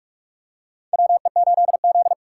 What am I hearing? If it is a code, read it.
Morse code, heard as a keyed tone, about 46 wpm, 700 Hz. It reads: WE8Z
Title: USA Call Signs - 4 Characters - 3x Character Spacing - 45wpm